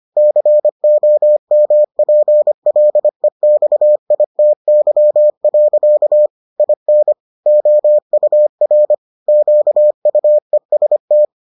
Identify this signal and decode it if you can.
Morse code, heard as a keyed tone, about 25 wpm, 600 Hz. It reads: COMPLEXITY. IN OUR QUEST